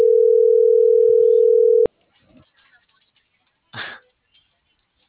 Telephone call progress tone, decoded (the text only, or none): ringback tone